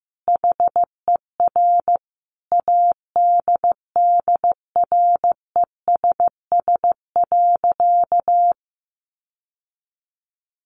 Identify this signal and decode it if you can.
Morse code, heard as a keyed tone, about 15 wpm, 700 Hz. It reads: HER ADDRESS.
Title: Sentences from Top 1100 Words 15wpm